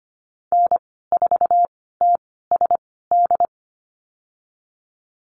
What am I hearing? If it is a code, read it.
Morse code, heard as a keyed tone, about 25 wpm, spaced as for 16 wpm, 700 Hz. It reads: N4TSD
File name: Lesson 15g - Number 5 - Call Signs 25wpm